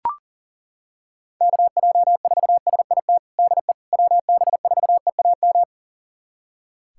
Morse code, heard as a keyed tone, about 40 wpm, 700 Hz. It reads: KJ4SIT DE WB4EAM
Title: USA Callsign DE Callsign - 6 Characters - 40wpm